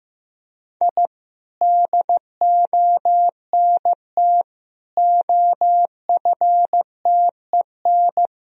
Morse code, heard as a keyed tone, about 15 wpm, 700 Hz. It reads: I DONT OFTEN
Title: Sentences from Top 1000 Words - Review Part 1 - 15wpm